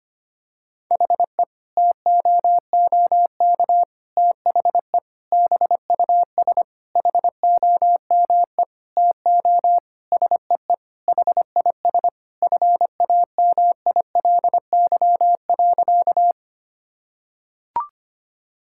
Morse code, heard as a keyed tone, about 25 wpm, 700 Hz. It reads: HE TOOK THE BUS HOME TO SEE HIS FAMILY.
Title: Sentences from Top 1000 Words - Review Part 6 - 25wpm